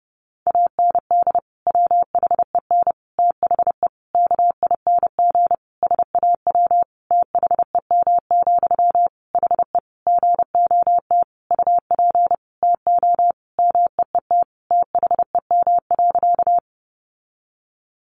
Morse code, heard as a keyed tone, about 30 wpm, 700 Hz. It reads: AND WHEN THE KING SAW THEM, HE GOT UP TO MEET THEM.
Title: Sentences from Top 1100 Words 30wpm